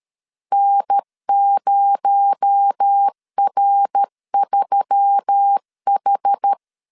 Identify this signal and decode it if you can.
Morse code, heard as a keyed tone, about 13 wpm, 800 Hz. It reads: N0R3H
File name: LEZIONI / 8 YJ45 / 8 YJ4 12wpm 2151